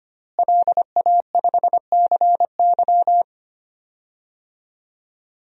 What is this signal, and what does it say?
Morse code, heard as a keyed tone, about 25 wpm, 700 Hz. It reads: LA5CY
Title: DXCC-100 Call Signs - 5 Characters - Repeated 3x - 25wpm